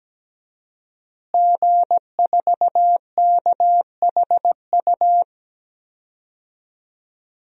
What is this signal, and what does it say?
Morse code, heard as a keyed tone, about 17 wpm, 700 Hz. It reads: G4KHU
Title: DXCC-100 Call Signs - 5 Characters - Repeated 3x - 17wpm